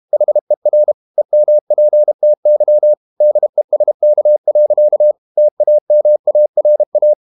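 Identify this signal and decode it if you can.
Morse code, heard as a keyed tone, about 32 wpm, 600 Hz. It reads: HER EMPTY DESK. TAMARA
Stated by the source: Morse code practice at 32 wpm from CW med Gnister, reading Ken Follett's novel Never in English.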